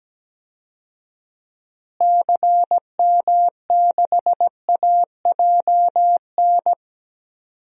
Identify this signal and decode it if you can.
Morse code, heard as a keyed tone, about 17 wpm, 700 Hz. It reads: CM6AJN
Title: DXCC-100 Call Signs - 6 Characters - Repeated 3x - 17wpm